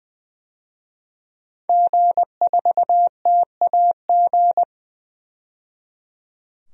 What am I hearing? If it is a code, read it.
Morse code, heard as a keyed tone, about 20 wpm, 700 Hz. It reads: G4TAG
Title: DXCC-100 Call Signs - 5 Characters - 20wpm